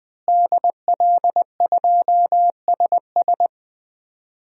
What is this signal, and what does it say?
Morse code, heard as a keyed tone, about 20 wpm, 700 Hz. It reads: DL2SS
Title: DXCC-100 Call Signs - 5 Characters - Repeated 3x - 20wpm